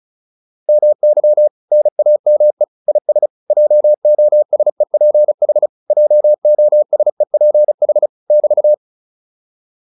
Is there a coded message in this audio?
Yes: Morse code from 0.7 to 8.7 s, heard as a keyed tone, about 34 wpm, 600 Hz. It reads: MY NAME IS JOSEPH JOSEPH <BT>